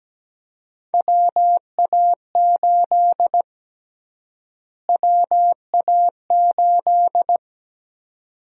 Morse code, heard as a keyed tone, about 17 wpm, 700 Hz. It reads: WA8 WA8
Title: USA Call Sign Prefixes - Repeated 2x - 17wpm